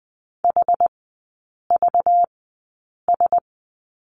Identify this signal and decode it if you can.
Morse code, heard as a keyed tone, about 20 wpm, spaced as for 8 wpm, 700 Hz. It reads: HVS
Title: 191 Lesson 31e - Letter v - Three Characters 20wpm